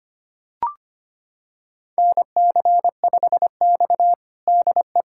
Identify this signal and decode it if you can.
Morse code, heard as a keyed tone, about 25 wpm, 700 Hz. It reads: NC5X DE
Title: USA Callsign DE Callsign - 4 Characters - 25wpm